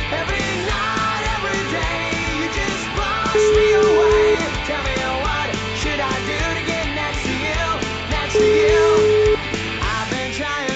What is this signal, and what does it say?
Ringback tone (Europe, 425 Hz), with noise.